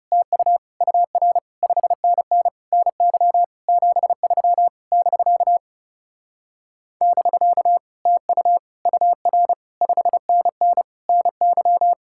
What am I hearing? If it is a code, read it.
Morse code, heard as a keyed tone, about 39 wpm, 700 Hz. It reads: TU UR 5NN NY 73 <BK> <BK> TU UR 5NN NY